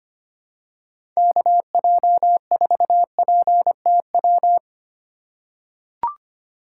Morse code, heard as a keyed tone, about 25 wpm, 700 Hz. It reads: KJ4PTW
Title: USA Call Signs - 6 Characters - 4x Character Spacing - 25wpm